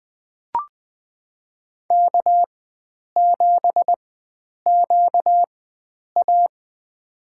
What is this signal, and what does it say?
Morse code, heard as a keyed tone, about 20 wpm, spaced as for 9 wpm, 700 Hz. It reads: K7QA
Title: USA Call Signs - 4 Characters - 4x Character Spacing - 20wpm